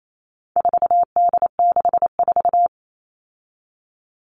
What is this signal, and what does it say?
Morse code, heard as a keyed tone, about 28 wpm, 700 Hz. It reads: 4D64
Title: All Cut Numbers - 4-Digit 28wpm